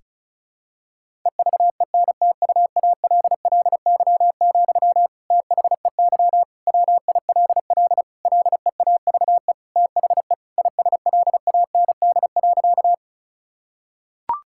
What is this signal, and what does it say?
Morse code, heard as a keyed tone, about 34 wpm, 700 Hz. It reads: EVENTUALLY, THEY WILL LEAVE THE ISLAND.
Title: Sentences from Top 1000 Words - Review Part 6 - 35wpm